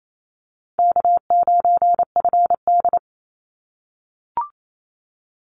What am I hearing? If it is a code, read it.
Morse code, heard as a keyed tone, about 28 wpm, 700 Hz. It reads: K9FD
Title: Easy Call Signs - 28wpm